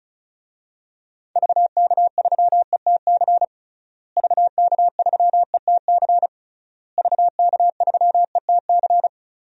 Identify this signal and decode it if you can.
Morse code, heard as a keyed tone, about 36 wpm, 700 Hz. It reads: VK3ETC VK3ETC VK3ETC
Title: DXCC-100 Call Signs - 6 Characters - Repeated 3x - 35wpm